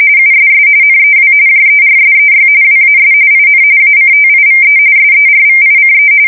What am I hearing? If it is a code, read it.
RTTY, about 45.45 baud. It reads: USS NEW JERSEY WAS IN ACTIVE SERVICE
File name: fsk170 / USS-New-Jersey